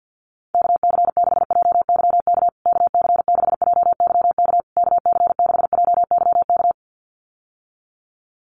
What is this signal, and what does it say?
Morse code, heard as a keyed tone, about 50 wpm, 700 Hz. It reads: KC6JYK KC6JYK KC6JYK
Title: USA Call Signs Repeated 3x 50wpm